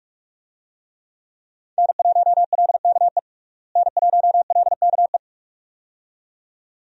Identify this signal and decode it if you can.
Morse code, heard as a keyed tone, about 41 wpm, 700 Hz. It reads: N1LKE N1LKE
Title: USA Call Signs - 5 Characters - Repeated 2x - 45wpm